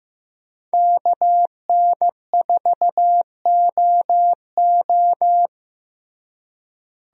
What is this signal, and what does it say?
Morse code, heard as a keyed tone, about 15 wpm, 700 Hz. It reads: KN4OO
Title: USA Call Signs - 5 Characters - Repeated 3x - 15wpm